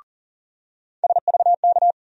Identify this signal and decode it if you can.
Morse code, heard as a keyed tone, about 40 wpm, 700 Hz. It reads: SVK
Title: USA Call Sign Postfixes - 3 Letters 40wpm